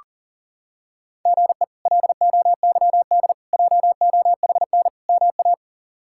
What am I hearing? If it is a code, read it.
Morse code, heard as a keyed tone, about 37 wpm, 700 Hz. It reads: GE LOYD JOHN MA
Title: K1USN SST QSO Practice - 40wpm